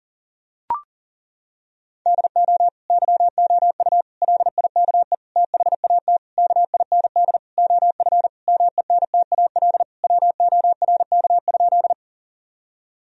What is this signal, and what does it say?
Morse code, heard as a keyed tone, about 40 wpm, 700 Hz. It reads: DO YOU LIKE THAT KIND OF MENTAL WORK?
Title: Sentences from Top 1000 Words - Review Part 4 - 40wpm